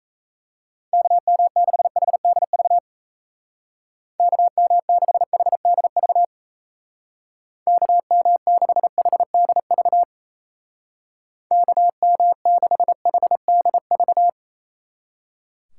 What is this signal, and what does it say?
Morse code, heard as a keyed tone, about 34 wpm, 700 Hz. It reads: KM6HDV KM6HDV KM6HDV KM6HDV
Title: Speed Racing - USA 2x3 Call Signs 28wpm